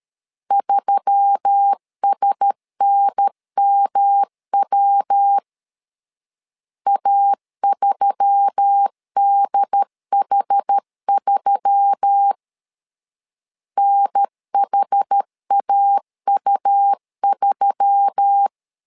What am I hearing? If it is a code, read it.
Morse code, heard as a keyed tone, about 13 wpm, 800 Hz. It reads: 3SNMW A3DH3 NHAU3
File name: LEZIONI / 4 Q1S3 12wpm 2023